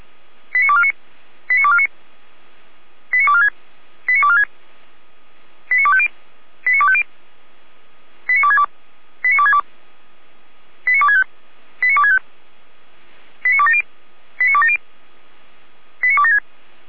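Five-tone selective call (ZVEI), heard as quick runs of five tones, tones of about 70 ms. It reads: 89249 89249 89246 89246 89250 89250 89262 89262 89265 89265 89270 89270 89276
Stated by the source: ZVEI five-tone selective call received on SDR, logged by Feuerwehr Orlamünde (f85360).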